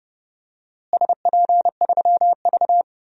Morse code, heard as a keyed tone, about 29 wpm, 700 Hz. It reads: SP3V
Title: DXCC-100 Call Signs - 4 Characters - Repeated 3x - 30wpm